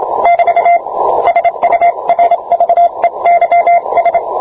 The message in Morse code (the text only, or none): <BT> SURVEYS